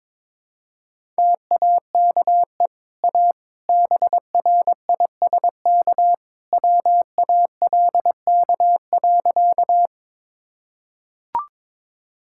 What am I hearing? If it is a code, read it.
Morse code, heard as a keyed tone, about 22 wpm, 700 Hz. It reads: TAKE A BRISK WALK.
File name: practice / 22wpm / Easy Sentences with Hints - 22wpm